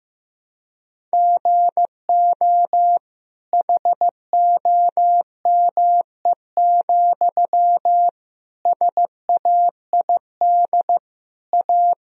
Morse code, heard as a keyed tone, about 15 wpm, 700 Hz. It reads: GO HOME, SAID A
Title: Sentences from Top 1000 Words - Review Part 1 - 15wpm